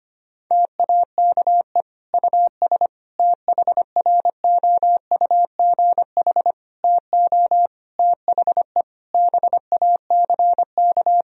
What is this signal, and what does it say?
Morse code, heard as a keyed tone, about 25 wpm, 700 Hz. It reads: TAKE US THROUGH TO THE BACK
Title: Sentences from Top 1000 Words - Review Part 1 - 25wpm